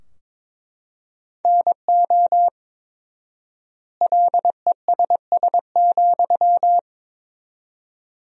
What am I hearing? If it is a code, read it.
Morse code, heard as a keyed tone, about 21 wpm, 700 Hz. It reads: NO LESS,
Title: 4x Word-Spacing - Sentences from Top 400 Words - 22wpm